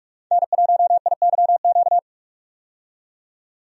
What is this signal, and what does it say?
Morse code, heard as a keyed tone, about 43 wpm, 700 Hz. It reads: N1IYQ
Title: USA Call Signs - 5 Characters - 45wpm